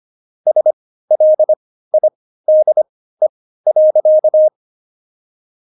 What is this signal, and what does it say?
Morse code, heard as a keyed tone, about 25 wpm, spaced as for 15 wpm, 620 Hz. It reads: SLIDE.
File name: Jokes-25-char-15-eff